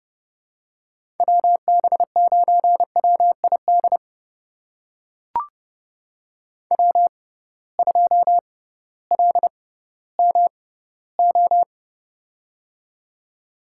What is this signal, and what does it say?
Morse code, heard as a keyed tone, about 30 wpm, spaced as for 16 wpm, 700 Hz. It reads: WB9WID W2LMO
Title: USA Call Signs - 6x Character Spacing - 30wpm